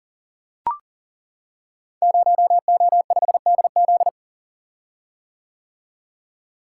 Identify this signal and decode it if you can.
Morse code, heard as a keyed tone, about 40 wpm, 700 Hz. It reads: 0O5DZ
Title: Code Groups - 5 Letter:Numbers - 40wpm